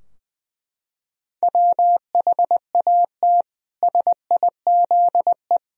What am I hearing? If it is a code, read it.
Morse code, heard as a keyed tone, about 21 wpm, 700 Hz. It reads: WHAT SIZE
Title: Sentences from Top 1000 Words - Review Part 2 - 20wpm